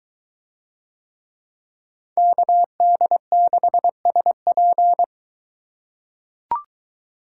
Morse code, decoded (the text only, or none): KD6SP